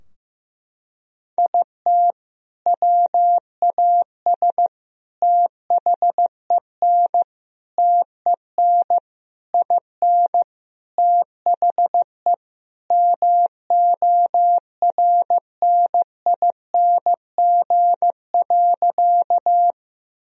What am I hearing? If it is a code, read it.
Morse code, heard as a keyed tone, about 15 wpm, 700 Hz. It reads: IT WAS THEN TEN IN THE MORNING.
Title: Sentences from Top 1000 Words - Review Part 2 - 15wpm